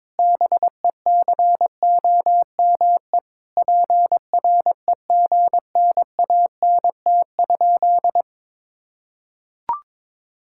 Morse code, heard as a keyed tone, about 22 wpm, 700 Hz. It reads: BECOME PREGNANT?
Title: Sentences from Top 1000 Words - Review Part 6 - 22wpm